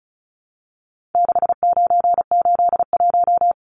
Morse code, heard as a keyed tone, about 34 wpm, 700 Hz. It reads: 6981
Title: Numbers - 4-Digit 35wpm